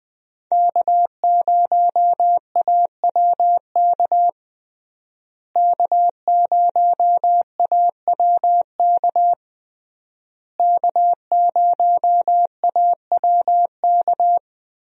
Morse code, heard as a keyed tone, about 20 wpm, 700 Hz. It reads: K0AWK K0AWK K0AWK
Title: USA Call Signs - 5 Characters - Repeated 3x - 20wpm